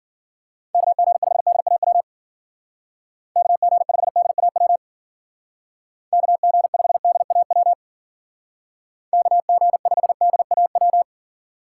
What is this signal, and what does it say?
Morse code, heard as a keyed tone, about 49 wpm, 700 Hz. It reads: KG5DAW KG5DAW KG5DAW KG5DAW
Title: Speed Racing - USA 2x3 Call Signs 40wpm